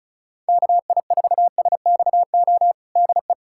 Morse code, heard as a keyed tone, about 34 wpm, 700 Hz. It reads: KI4SXO DE
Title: USA Callsign DE Callsign 35wpm